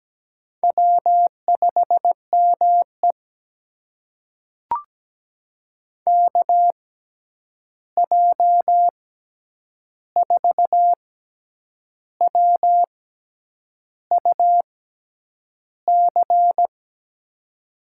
Morse code, heard as a keyed tone, about 17 wpm, spaced as for 8 wpm, 700 Hz. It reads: W5ME KJ4WUC
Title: USA Call Signs - 6x Character Spacing - 17wpm